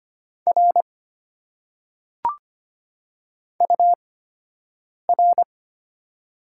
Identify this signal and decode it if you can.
Morse code, heard as a keyed tone, about 25 wpm, spaced as for 7 wpm, 700 Hz. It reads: R UR
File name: Top 90 QSO Elements - 8x Character Spacing - 25wpm